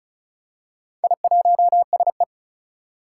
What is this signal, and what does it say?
Morse code, heard as a keyed tone, about 34 wpm, 700 Hz. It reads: I1SE
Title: Four Challenging Letter-Number Combinations 35wpm